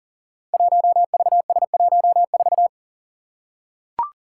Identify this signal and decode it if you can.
Morse code, heard as a keyed tone, about 39 wpm, 700 Hz. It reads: 1VS14
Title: Five Challenging Letter-Number Combinations 40wpm